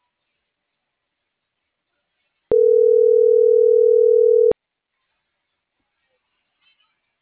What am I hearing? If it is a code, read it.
Ringback tone (North America), 440 + 480 Hz.